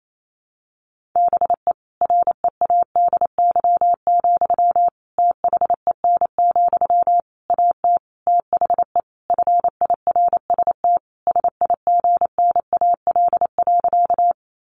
Morse code, heard as a keyed tone, about 28 wpm, 700 Hz. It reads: BE READY, THEN, AT THE FIRST SIGNAL.
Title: Sentences from Top 1400 Words 28wpm